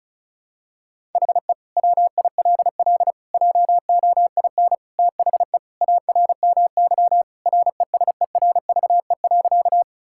Morse code, heard as a keyed tone, about 36 wpm, 700 Hz. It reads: HE WILL JOIN THE ARMY RESERVE.
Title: Sentences from Top 1000 Words - Review Part 6 - 35wpm